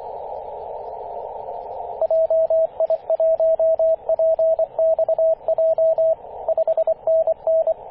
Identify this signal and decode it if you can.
Morse code, heard as a keyed tone, about 24 wpm, 650 Hz, with radio noise. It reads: JI1PXJ 5NN
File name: posts / 2016-apr-23-0243z-t88fg-18-085mhz-cw